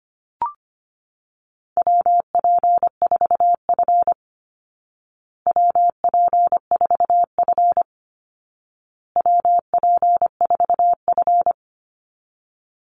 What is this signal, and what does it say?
Morse code, heard as a keyed tone, about 25 wpm, 700 Hz. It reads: WP4F WP4F WP4F
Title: Call Signs Repeated 2x - 25wpm